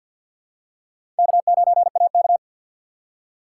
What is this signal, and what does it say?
Morse code, heard as a keyed tone, about 47 wpm, 700 Hz. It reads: K9AK
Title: USA Call Signs - 4 Characters - 2x Character Spacing - 50wpm